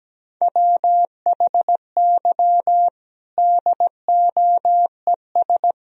Morse code, heard as a keyed tone, about 17 wpm, 700 Hz. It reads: WHY DOES
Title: Sentences from Top 1000 Words - Review Part 4 - 17wpm